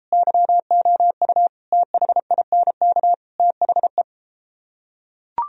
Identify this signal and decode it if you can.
Morse code, heard as a keyed tone, about 32 wpm, 700 Hz. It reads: YOU THINK THE